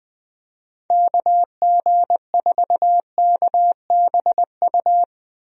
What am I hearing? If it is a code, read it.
Morse code, heard as a keyed tone, about 20 wpm, 700 Hz. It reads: KG4KBU
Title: USA Call Signs - 6 Characters - Repeated 2x - 20wpm